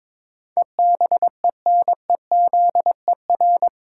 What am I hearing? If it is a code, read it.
Morse code, heard as a keyed tone, about 21 wpm, 700 Hz. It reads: EBENEZER
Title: Pangram Challenge v2 - 22wpm